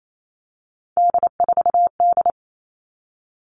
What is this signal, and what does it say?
Morse code, heard as a keyed tone, about 28 wpm, 700 Hz. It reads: D4D
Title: All Cut Numbers - 3-Digit 28wpm